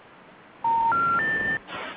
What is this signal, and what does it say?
Special information tone, with noise.